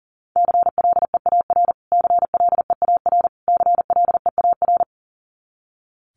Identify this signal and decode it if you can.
Morse code, heard as a keyed tone, about 40 wpm, 700 Hz. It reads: CLEAR CLEAR CLEAR
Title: Top 500 Words Repeated 40wpm